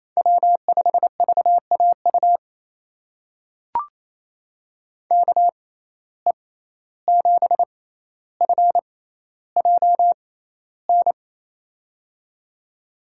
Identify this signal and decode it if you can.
Morse code, heard as a keyed tone, about 28 wpm, spaced as for 14 wpm, 700 Hz. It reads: W5VAU KE7FJN
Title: USA Call Signs - 6x Character Spacing - 28wpm